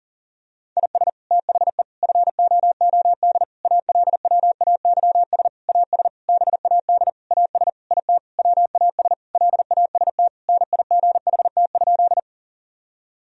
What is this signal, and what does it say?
Morse code, heard as a keyed tone, about 40 wpm, 700 Hz. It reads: IS THE FOOD ALWAYS AS BAD AS IT WAS LAST NIGHT?
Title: Sentences from Top 1000 Words - Review Part 1 - 40wpm